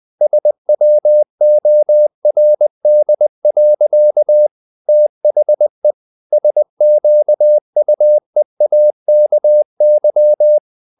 Morse code, heard as a keyed tone, about 20 wpm, 600 Hz. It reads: SWORD. THE SQUEAKY
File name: CW / Practice / EnglishSayings20